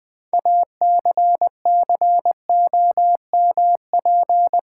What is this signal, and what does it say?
Morse code, heard as a keyed tone, about 20 wpm, 700 Hz. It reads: ACCOMP
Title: Build A Word - 3 Letters At A Time - 9-Letter Words - 20wpm